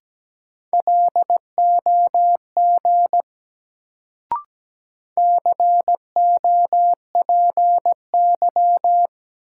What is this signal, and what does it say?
Morse code, heard as a keyed tone, about 17 wpm, 700 Hz. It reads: LOG COPY
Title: Sentences from Top 1400 Words - New Word Drill - 17wpm